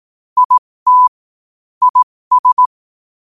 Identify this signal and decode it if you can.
Morse code, heard as a keyed tone, about 18 wpm, spaced as for 15 wpm, 1000 Hz. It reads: IT IS